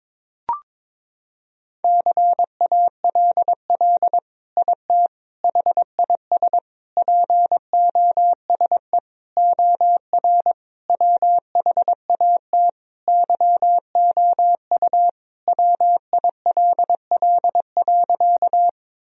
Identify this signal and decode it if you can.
Morse code, heard as a keyed tone, about 22 wpm, 700 Hz. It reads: CALL IT HIS POSE OR WHAT YOU WILL.
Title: Sentences from Top 1000 Words - Review Part 2 - 22wpm